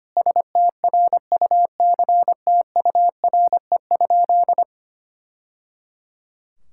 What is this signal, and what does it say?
Morse code, heard as a keyed tone, about 25 wpm, 700 Hz. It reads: STRUCTURE?
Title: Sentences from Top 1000 Words - Review Part 6 - 25wpm